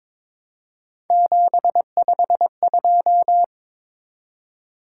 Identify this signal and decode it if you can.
Morse code, heard as a keyed tone, about 22 wpm, 700 Hz. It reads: 752
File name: Three Letter-Number Combinations - 22wpm